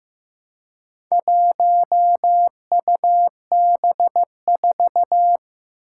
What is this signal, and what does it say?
Morse code, heard as a keyed tone, about 15 wpm, 700 Hz. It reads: 1UB4
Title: Four Challenging Letter-Number Combinations 15wpm